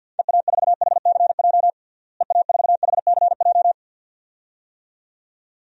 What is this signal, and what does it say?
Morse code, heard as a keyed tone, about 52 wpm, 700 Hz. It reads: EA4HCJ EA4HCJ
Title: DXCC-100 Call Signs - 6 Characters - Repeated 2x - 50wpm